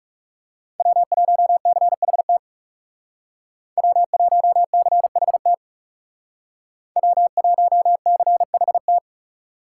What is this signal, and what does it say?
Morse code, heard as a keyed tone, about 39 wpm, 700 Hz. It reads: W1CHT W1CHT W1CHT